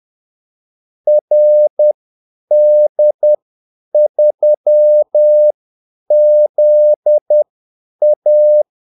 Morse code, heard as a keyed tone, about 10 wpm, spaced as for 8 wpm, 600 Hz. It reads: RD3ZA
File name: Callsigns-10-8wpm